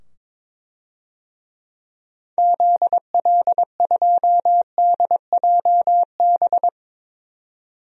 Morse code, heard as a keyed tone, about 21 wpm, 700 Hz. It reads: ZL2DJB
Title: DXCC-100 Call Signs - 6 Characters - 22wpm